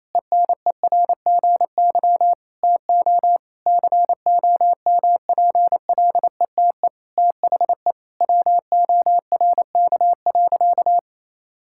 Morse code, heard as a keyed tone, about 28 wpm, 700 Hz. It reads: ENERGY TO COMPLETE THE WORK.
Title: Sentences from Top 1000 Words - Review Part 6 - 28wpm